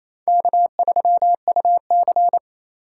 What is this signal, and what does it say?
Morse code, heard as a keyed tone, about 32 wpm, 700 Hz. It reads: K3UC
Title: Speed Racing - USA 1x2 Call Signs 28wpm